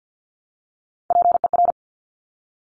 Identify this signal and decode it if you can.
Morse code, heard as a keyed tone, about 50 wpm, 700 Hz. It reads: PER
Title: Top Three Letter Combinations in Words 50wpm